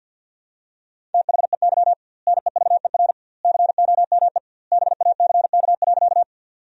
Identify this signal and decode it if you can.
Morse code, heard as a keyed tone, about 47 wpm, 700 Hz. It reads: THEY NEVER COME BACK.
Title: 2x Word-Spacing - Sentences from Top 300 Words - 50wpm